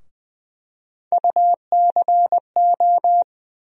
Morse code, heard as a keyed tone, about 21 wpm, 700 Hz. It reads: UCO
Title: USA Call Sign Postfixes - 3 Letters 20wpm